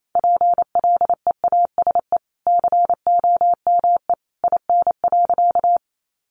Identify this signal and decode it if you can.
Morse code, heard as a keyed tone, about 28 wpm, 700 Hz. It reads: PLEASE COME IN.
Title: Sentences from Top 1500 Words 28wpm